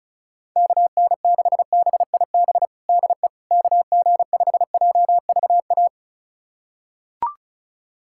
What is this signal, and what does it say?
Morse code, heard as a keyed tone, about 34 wpm, 700 Hz. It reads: KN6BIB DE KG5JVA
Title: USA Callsign DE Callsign - 6 Characters - 35wpm